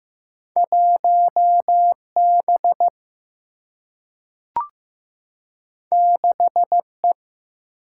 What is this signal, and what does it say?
Morse code, heard as a keyed tone, about 15 wpm, 700 Hz. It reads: 1B 6E
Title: Single-Letter Single-Number 15wpm